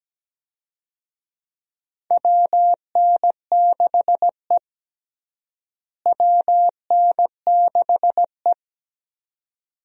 Morse code, heard as a keyed tone, about 17 wpm, 700 Hz. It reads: WN6E WN6E
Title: USA Call Signs - 4 Characters - Repeated 2x - 17wpm